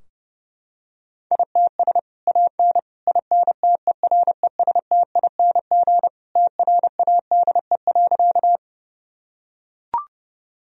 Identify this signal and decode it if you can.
Morse code, heard as a keyed tone, about 30 wpm, 700 Hz. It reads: ITS AN INTERESTING TRADE.
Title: Sentences from Top 1000 Words - Review Part 4 - 30wpm